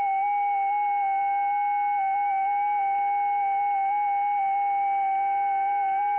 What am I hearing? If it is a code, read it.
FT8.
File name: IC-7100